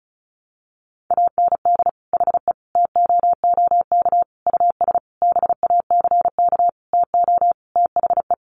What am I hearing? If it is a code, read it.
Morse code, heard as a keyed tone, about 34 wpm, 700 Hz. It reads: AND HE TOOK US BACK TO THE
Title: Sentences from Top 1500 Words 35wpm